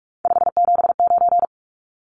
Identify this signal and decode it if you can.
Morse code, heard as a keyed tone, about 44 wpm, 700 Hz. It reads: 579
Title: Numbers - 3-Digit 45wpm